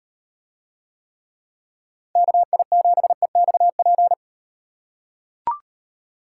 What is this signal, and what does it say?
Morse code, heard as a keyed tone, about 38 wpm, 700 Hz. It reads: KI7EXP